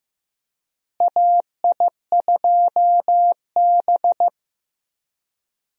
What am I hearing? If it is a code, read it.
Morse code, heard as a keyed tone, about 15 wpm, 700 Hz. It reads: AI2B
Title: USA Call Signs - 4 Characters - Repeated 2x - 15wpm